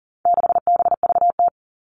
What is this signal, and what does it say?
Morse code, heard as a keyed tone, about 40 wpm, 700 Hz. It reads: 6BVT
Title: All Cut Numbers - 4-Digit 40wpm